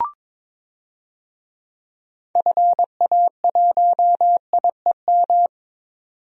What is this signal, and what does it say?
Morse code, heard as a keyed tone, about 22 wpm, 700 Hz. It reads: FA1IEM
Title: DXCC-100 Call Signs - 6 Characters - 22wpm